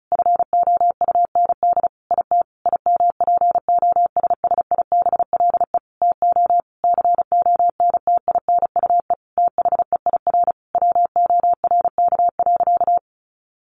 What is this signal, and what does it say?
Morse code, heard as a keyed tone, about 34 wpm, 700 Hz. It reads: FOUND IT IMPOSSIBLE TO CONTINUE THEIR WORK.
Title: Sentences from Top 1100 Words 35wpm